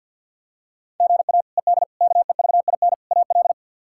Morse code, heard as a keyed tone, about 46 wpm, 700 Hz. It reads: GA ED KEVIN AL
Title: K1USN SST QSO Practice - 50wpm